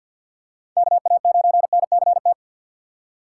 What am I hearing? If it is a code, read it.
Morse code, heard as a keyed tone, about 52 wpm, 700 Hz. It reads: KA9NCT
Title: USA Call Signs - 6 Characters - 4x Character Spacing - 50wpm